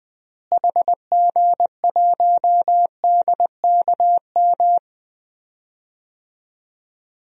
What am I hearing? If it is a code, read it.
Morse code, heard as a keyed tone, about 20 wpm, 700 Hz. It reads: HG1DKM
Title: DXCC-100 Call Signs - 6 Characters - Repeated 3x - 20wpm